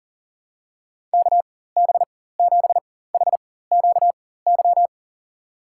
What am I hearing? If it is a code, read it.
Morse code, heard as a keyed tone, about 40 wpm, spaced as for 19 wpm, 700 Hz. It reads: KB7HQY
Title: USA Call Signs - 4x Character Spacing - 40wpm